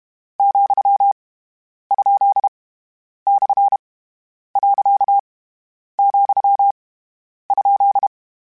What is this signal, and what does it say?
Morse code, heard as a keyed tone, about 32 wpm, 800 Hz. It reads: , ? / . , ?
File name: Misc / Misc-mp3s / Punct @30